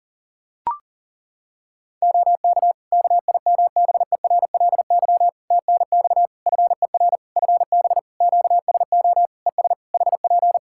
Morse code, heard as a keyed tone, about 37 wpm, 700 Hz. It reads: OK KIMBERLY TNX FER FB QSO ES HP
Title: Practice QSO - Exchange 4 - Ending - 40wpm